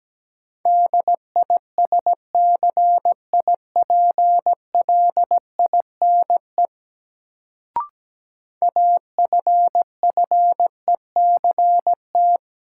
Morse code, heard as a keyed tone, about 17 wpm, 700 Hz. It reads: DISCIPLINE AFFECT
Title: Sentences from Top 900 Words - New Word Drill - 17wpm